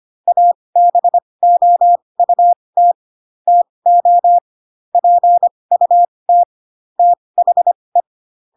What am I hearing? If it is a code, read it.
Morse code, heard as a keyed tone, about 25 wpm, spaced as for 20 wpm, 700 Hz. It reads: ABOUT TO PUT THE
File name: Pinocchio / Pinocchio ch3 15 25 ik1hge